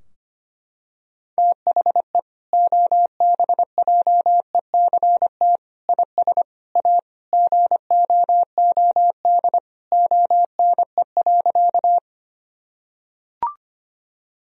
Morse code, heard as a keyed tone, about 25 wpm, 700 Hz. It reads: THE OBJECT IS A GOOD ONE.
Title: Sentences from Top 1000 Words - Review Part 3 - 25wpm